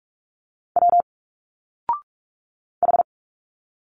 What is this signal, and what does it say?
Morse code, heard as a keyed tone, about 42 wpm, 700 Hz. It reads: W H
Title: Single Letters 45wpm